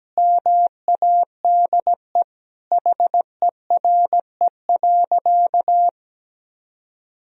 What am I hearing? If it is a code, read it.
Morse code, heard as a keyed tone, about 17 wpm, 700 Hz. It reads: MADE HERE.
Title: Sentences from Top 1000 Words - Review Part 4 - 17wpm